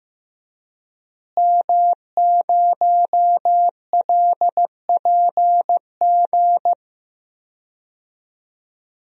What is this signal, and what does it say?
Morse code, heard as a keyed tone, about 15 wpm, 700 Hz. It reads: M0LPG
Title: DXCC-100 Call Signs - 5 Characters - Repeated 2x - 15wpm